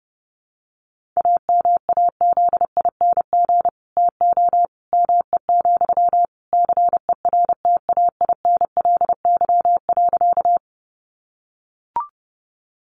Morse code, heard as a keyed tone, about 30 wpm, 700 Hz. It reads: AMAZING TO ME, CERTAINLY.
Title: Sentences from Top 1500 Words 30wpm